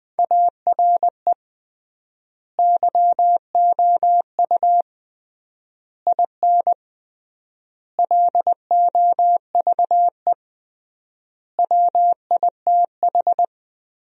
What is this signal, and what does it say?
Morse code, heard as a keyed tone, about 21 wpm, 700 Hz. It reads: ARE YOU IN LOVE WITH
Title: 3x Word-Spacing - Sentences from Top 400 Words - 20wpm